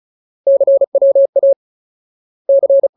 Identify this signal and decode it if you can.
Morse code, heard as a keyed tone, about 37 wpm, 550 Hz. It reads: CWA C
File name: Ultimate Word List / 35WPM Ultimate